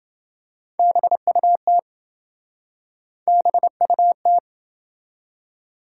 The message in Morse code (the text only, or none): BUT BUT